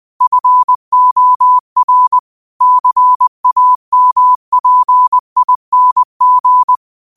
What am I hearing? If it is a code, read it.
Morse code, heard as a keyed tone, about 20 wpm, 1000 Hz. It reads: FOR CAMPING
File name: qotdfiles / 20wpm